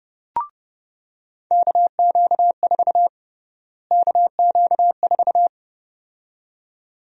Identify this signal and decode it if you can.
Morse code, heard as a keyed tone, about 32 wpm, 700 Hz. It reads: KQ4 KQ4
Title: USA Call Sign Prefixes - Repeated 2x - 30wpm